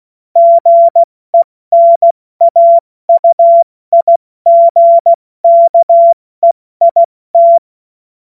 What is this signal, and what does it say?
Morse code, heard as a keyed tone, about 16 wpm, spaced as for 14 wpm, 680 Hz. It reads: GENAUIGKEIT
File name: Morsekurs / Witze / Witz 21. Juni 2022 70 F80